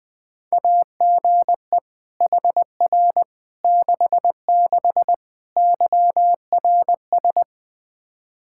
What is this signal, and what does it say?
Morse code, heard as a keyed tone, about 20 wpm, 700 Hz. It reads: AGE HR 66 YRS
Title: Practice QSO - Exchange 3 - More Info - 20wpm